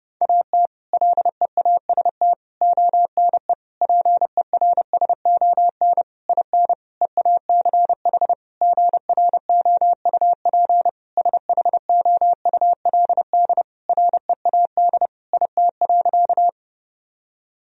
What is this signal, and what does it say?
Morse code, heard as a keyed tone, about 30 wpm, 700 Hz. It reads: AT LEAST ONE PERSON IN EACH GROUP SHOULD READ IT.
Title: Sentences from Top 1000 Words - Review Part 1 - 30wpm